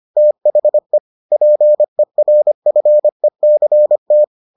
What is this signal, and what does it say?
Morse code, heard as a keyed tone, about 25 wpm, 600 Hz. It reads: THE PERFECT